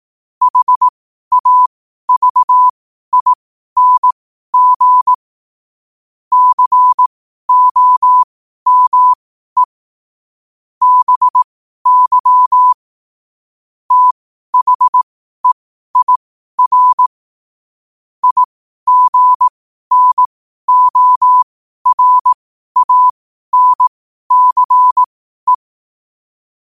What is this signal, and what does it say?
Morse code, heard as a keyed tone, about 18 wpm, spaced as for 12 wpm, 1000 Hz. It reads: HAVING COME BY THEIR IGNORANCE
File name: qotdfiles / 12wpm